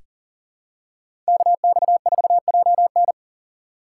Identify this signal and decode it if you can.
Morse code, heard as a keyed tone, about 40 wpm, 700 Hz. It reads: KX4JN
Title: USA Call Signs - 5 Characters - 3x Character Spacing - 40wpm